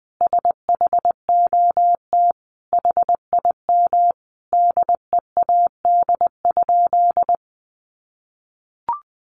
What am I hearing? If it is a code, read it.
Morse code, heard as a keyed tone, about 20 wpm, 700 Hz. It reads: SHOT HIM DEAD?
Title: Sentences from Top 1500 Words 20wpm